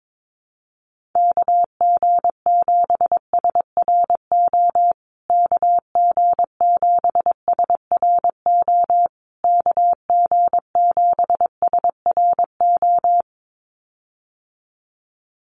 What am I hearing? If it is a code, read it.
Morse code, heard as a keyed tone, about 22 wpm, 700 Hz. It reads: KG7SRO KG7SRO KG7SRO
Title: USA Call Signs Repeated 3x 22wpm